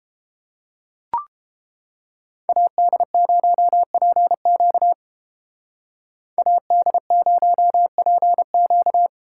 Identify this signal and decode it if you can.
Morse code, heard as a keyed tone, about 31 wpm, 700 Hz. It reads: AD0PQ AD0PQ